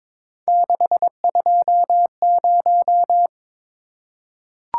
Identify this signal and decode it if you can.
Morse code, heard as a keyed tone, about 23 wpm, 700 Hz. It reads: 620
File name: practice / 22wpm / Three Letter-Number Combinations - 22wpm